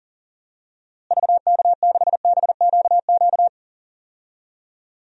Morse code, heard as a keyed tone, about 40 wpm, 700 Hz. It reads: VK6BQQ
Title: DXCC-100 Call Signs - 6 Characters - Repeated 2x - 40wpm